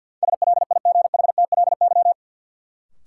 Morse code, heard as a keyed tone, about 52 wpm, 700 Hz. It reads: SLIGHTLY
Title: Sentences from Top 1300 Words - New Word Drill - 50wpm